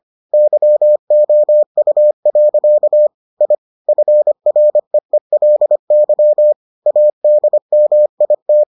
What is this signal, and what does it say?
Morse code, heard as a keyed tone, about 25 wpm, 600 Hz. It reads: YOU. I FREELY ADMIT